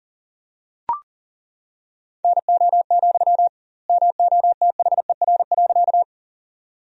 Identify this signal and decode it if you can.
Morse code, heard as a keyed tone, about 40 wpm, 700 Hz. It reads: NO, MOTHER.
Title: 2x Word-Spacing - Sentences from Top 200 Words - 40wpm